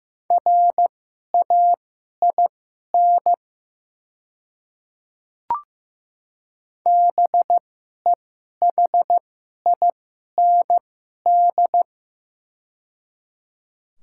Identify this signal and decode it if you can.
Morse code, heard as a keyed tone, about 15 wpm, spaced as for 11 wpm, 700 Hz. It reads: RAIN BEHIND
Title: 2x Character Spacing - Top 500 Words - 15wpm